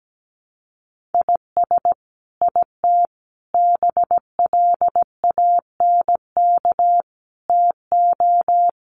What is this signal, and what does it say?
Morse code, heard as a keyed tone, about 17 wpm, 700 Hz. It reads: IS IT BLANK TO
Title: Sentences from Top 1500 Words 17wpm